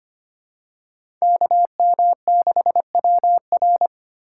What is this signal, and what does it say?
Morse code, heard as a keyed tone, about 25 wpm, 700 Hz. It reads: KM6WR
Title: USA Call Signs - 5 Characters - 2x Character Spacing - 25wpm